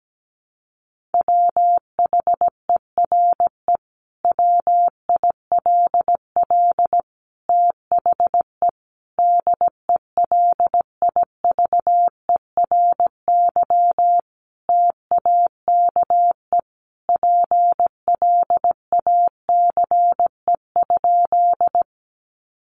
Morse code, heard as a keyed tone, about 17 wpm, 700 Hz. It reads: WHERE WILL THE DELIVERY TAKE PLACE?
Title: Sentences from Top 1500 Words 17wpm